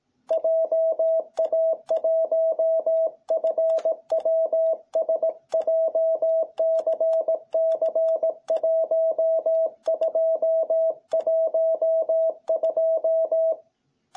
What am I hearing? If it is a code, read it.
Morse code, heard as a keyed tone, about 18 wpm, 660 Hz. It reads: JA1FWS JCC1212